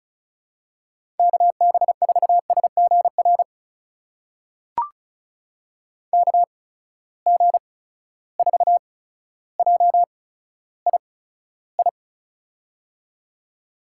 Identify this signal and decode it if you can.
Morse code, heard as a keyed tone, about 36 wpm, spaced as for 15 wpm, 700 Hz. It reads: KB4SGR KG4JII
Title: USA Call Signs - 6 Characters - 8x Character Spacing - 35wpm